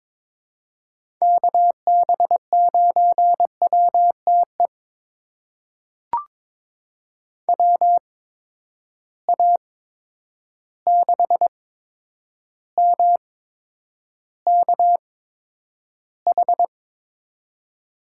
Morse code, heard as a keyed tone, about 22 wpm, spaced as for 9 wpm, 700 Hz. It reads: KB9WTE WA6MKH
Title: USA Call Signs - 6 Characters - 8x Character Spacing - 22wpm